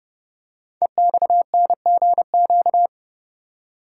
Morse code, heard as a keyed tone, about 30 wpm, 700 Hz. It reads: EXNGQ
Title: Code Groups - 5 Letters - 30wpm